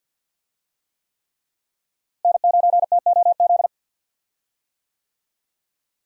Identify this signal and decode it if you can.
Morse code, heard as a keyed tone, about 46 wpm, 700 Hz. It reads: N9TOZ
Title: USA Call Signs - 5 Characters - Repeated 3x - 50wpm